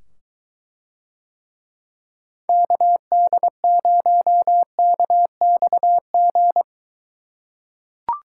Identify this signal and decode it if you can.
Morse code, heard as a keyed tone, about 22 wpm, 700 Hz. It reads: KD0KXG